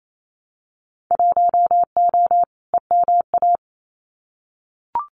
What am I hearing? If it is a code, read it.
Morse code, heard as a keyed tone, about 27 wpm, 700 Hz. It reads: 1O EMA
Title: Winter Field Day Exchanges Encore 28wpm